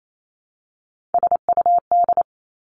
Morse code, heard as a keyed tone, about 28 wpm, 700 Hz. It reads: SUD
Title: Three Challenging Letter-Number Combinations 28wpm